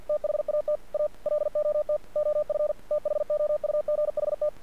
Morse code, heard as a keyed tone, about 47 wpm, 600 Hz. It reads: THAT A LOT OF THOUGHT
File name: archived / CRSnet